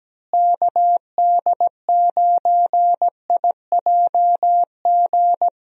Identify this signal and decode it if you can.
Morse code, heard as a keyed tone, about 17 wpm, 700 Hz. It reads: KD9IJG
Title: USA Call Signs - 6 Characters - Repeated 3x - 17wpm